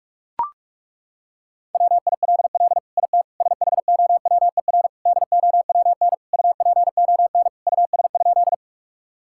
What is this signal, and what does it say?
Morse code, heard as a keyed tone, about 41 wpm, 700 Hz. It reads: WILL IT SHOWER DOWN UPON US?
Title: Sentences from Top 1600 Words 45wpm